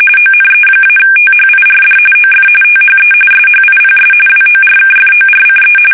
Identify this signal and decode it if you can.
RTTY, about 45.45 baud. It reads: ONE-THIRD OF THE SHIPS WORKFORCE
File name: fsk850 / battleshipnj